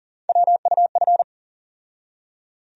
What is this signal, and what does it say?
Morse code, heard as a keyed tone, about 40 wpm, 700 Hz. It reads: WUF
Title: USA Call Sign Postfixes - 3 Letters 40wpm